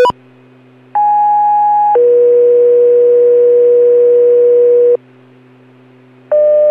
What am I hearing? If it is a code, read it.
Two-tone page, heard as a one-second tone, then a three-second tone, about 830 then 480 Hz. It is heard from 0.9 to 5.0 s.